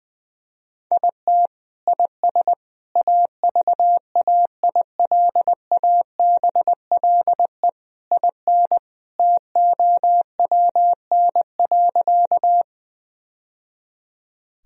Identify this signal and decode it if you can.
Morse code, heard as a keyed tone, about 21 wpm, 700 Hz. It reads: IT IS AVAILABLE IN TOWN.
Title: Sentences from Top 1000 Words - Review Part 3 - 20wpm